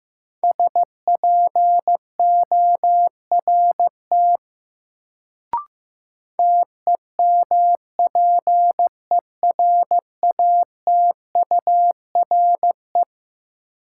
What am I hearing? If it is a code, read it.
Morse code, heard as a keyed tone, about 15 wpm, 700 Hz. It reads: SPORT TEMPERATURE
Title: Sentences from Top 1500 Words - New Word Drill Alt - 15wpm